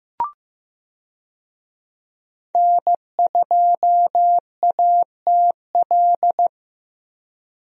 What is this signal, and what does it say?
Morse code, heard as a keyed tone, about 15 wpm, 700 Hz. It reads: N2ATL
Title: USA Call Signs - 5 Characters - Repeated 3x - 15wpm